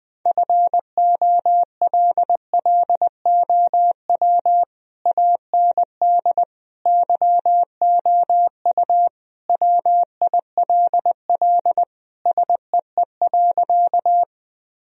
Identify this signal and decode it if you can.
Morse code, heard as a keyed tone, about 20 wpm, 700 Hz. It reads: FOLLOW AND YOU WILL SEE.
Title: Sentences from Top 1000 Words - Review Part 1 - 20wpm